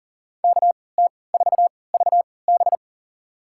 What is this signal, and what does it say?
Morse code, heard as a keyed tone, about 40 wpm, spaced as for 23 wpm, 700 Hz. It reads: KT4VB
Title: USA Call Signs - 5 Characters - 3x Character Spacing - 40wpm